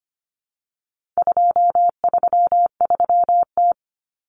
Morse code, heard as a keyed tone, about 25 wpm, 700 Hz. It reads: 233T
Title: Common Cut Numbers - 4-Digit 25wpm